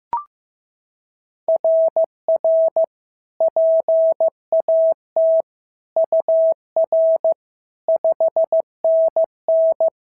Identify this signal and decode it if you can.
Morse code, heard as a keyed tone, about 14 wpm, 650 Hz. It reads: RR PAT UR 5NN
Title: SOTA QSO Practice 15wpm